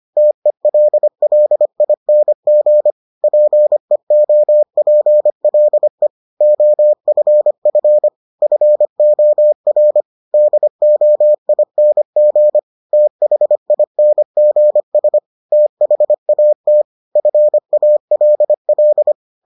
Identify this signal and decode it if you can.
Morse code, heard as a keyed tone, about 25 wpm, 600 Hz. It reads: TELLING PEOPLE OFF FOR DOING THINGS THAT FALL